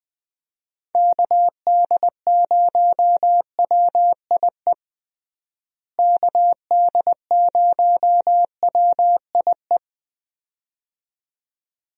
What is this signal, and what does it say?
Morse code, heard as a keyed tone, about 20 wpm, 700 Hz. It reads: KD0WIE KD0WIE
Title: USA Call Signs - 6 Characters - Repeated 2x - 20wpm